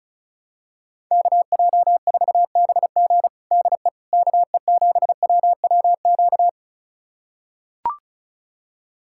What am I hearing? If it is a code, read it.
Morse code, heard as a keyed tone, about 34 wpm, 700 Hz. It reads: KJ4BG DE KE7WWQ
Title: USA Callsign DE Callsign 35wpm